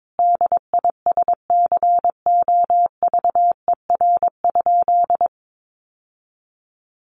Morse code, heard as a keyed tone, about 22 wpm, 700 Hz. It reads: DISCOVER?
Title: Sentences from Top 1100 Words 22wpm